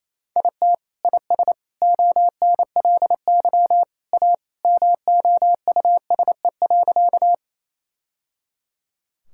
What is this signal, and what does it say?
Morse code, heard as a keyed tone, about 28 wpm, 700 Hz. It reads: IT IS ONLY A MOUSE.
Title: Sentences from Top 1000 Words - Review Part 5 - 28wpm